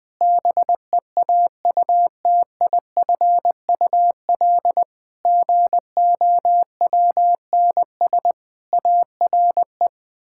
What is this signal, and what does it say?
Morse code, heard as a keyed tone, about 20 wpm, 700 Hz. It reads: BEAUTIFUL GOWNS ARE
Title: Pangram Challenge v2 - 20wpm